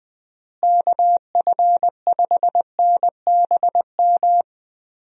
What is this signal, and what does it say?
Morse code, heard as a keyed tone, about 21 wpm, 700 Hz. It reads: KF5NBM
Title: USA Call Signs - 6 Characters - Repeated 2x - 20wpm